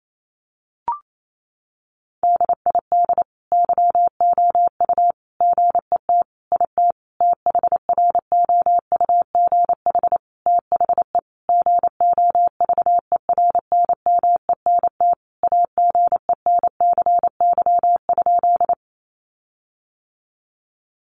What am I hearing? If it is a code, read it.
Morse code, heard as a keyed tone, about 28 wpm, 700 Hz. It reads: DID YOU GET IT THROUGH THE GOVERNMENT AGENCY?
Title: Sentences from Top 1300 Words 28wpm